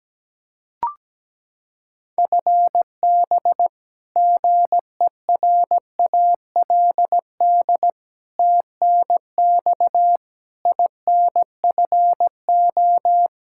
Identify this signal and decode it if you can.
Morse code, heard as a keyed tone, about 17 wpm, 700 Hz. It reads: FB GERALD TNX INFO
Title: Practice QSO - Exchange 3 - More Info - 17wpm